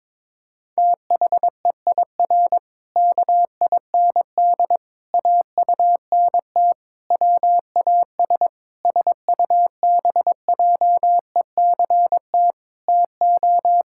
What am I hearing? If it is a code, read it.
Morse code, heard as a keyed tone, about 21 wpm, 700 Hz. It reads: THEIR KIND AUNT WAS SUBJECT TO
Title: Pangram Challenge v2 - 22wpm